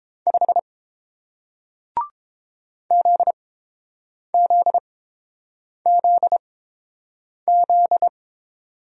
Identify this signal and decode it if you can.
Morse code, heard as a keyed tone, about 30 wpm, spaced as for 7 wpm, 700 Hz. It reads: 5 ZZZZ